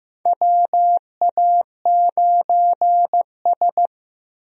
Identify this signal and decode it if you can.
Morse code, heard as a keyed tone, about 15 wpm, 700 Hz. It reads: WA9S
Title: USA Call Signs - 4 Characters - Repeated 3x - 15wpm